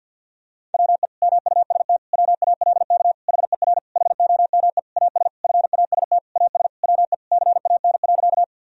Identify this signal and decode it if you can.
Morse code, heard as a keyed tone, about 51 wpm, 700 Hz. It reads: WE MUST WALK HER HOME AS FAST AS WE CAN.
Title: Sentences from Top 1000 Words - Review Part 2 - 50wpm